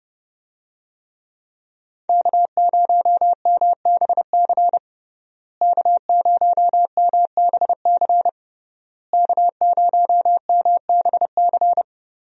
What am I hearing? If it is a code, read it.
Morse code, heard as a keyed tone, about 30 wpm, 700 Hz. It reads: K0MBC K0MBC K0MBC
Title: USA Call Signs - 5 Characters - Repeated 3x - 30wpm